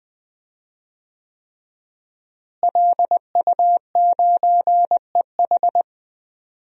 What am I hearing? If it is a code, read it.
Morse code, heard as a keyed tone, about 20 wpm, 700 Hz. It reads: LU9EH
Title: DXCC-100 Call Signs - 5 Characters - Repeated 3x - 20wpm